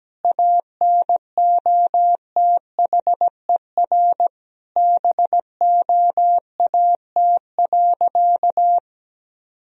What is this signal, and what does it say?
Morse code, heard as a keyed tone, about 17 wpm, 700 Hz. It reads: ANOTHER BOAT.
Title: Sentences from Top 1000 Words - Review Part 3 - 17wpm